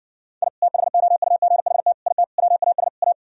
Morse code, heard as a keyed tone, about 59 wpm, 700 Hz. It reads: I THOUGHT IT WAS A